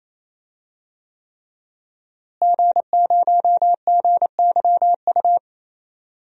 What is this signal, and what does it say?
Morse code, heard as a keyed tone, about 28 wpm, 700 Hz. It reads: G0GYU
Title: DXCC-100 Call Signs - 5 Characters - Repeated 3x - 28wpm